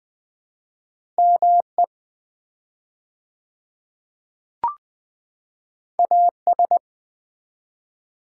Morse code, heard as a keyed tone, about 20 wpm, 700 Hz. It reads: ME AS
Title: 4x Word-Spacing - Sets of 2 Words - 20wpm